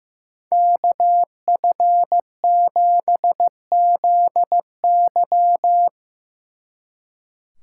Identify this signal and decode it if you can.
Morse code, heard as a keyed tone, about 16 wpm, 700 Hz. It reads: KF7ZY
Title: Speed Racing - USA 2x2 Call Signs 15wpm